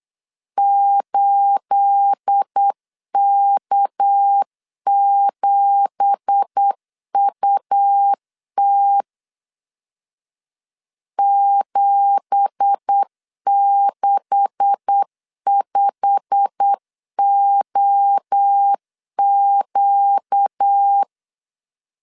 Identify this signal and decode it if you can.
Morse code, heard as a keyed tone, about 8 wpm, 800 Hz. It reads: 8K7UT 765OQ
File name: 8wpm